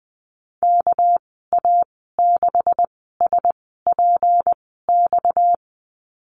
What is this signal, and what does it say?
Morse code, heard as a keyed tone, about 20 wpm, spaced as for 14 wpm, 700 Hz. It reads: KA6SPX
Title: 245 Lesson 39g - Letter z - Call Signs 20wpm